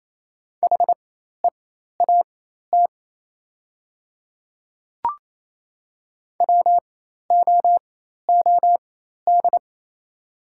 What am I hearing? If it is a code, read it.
Morse code, heard as a keyed tone, about 28 wpm, spaced as for 13 wpm, 700 Hz. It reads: HEAT WOOD
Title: 4x Character Spacing - Top 500 Words - 28wpm